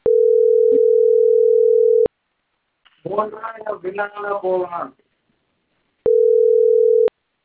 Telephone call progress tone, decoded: ringback tone